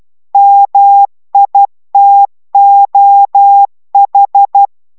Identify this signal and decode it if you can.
Morse code, heard as a keyed tone, about 12 wpm, 800 Hz. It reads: MITOH